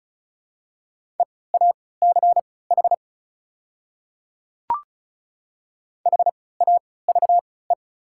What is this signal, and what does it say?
Morse code, heard as a keyed tone, about 36 wpm, spaced as for 20 wpm, 690 Hz. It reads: EACH HAVE
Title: 3x Character Spacing - Top 100 Words - 35wpm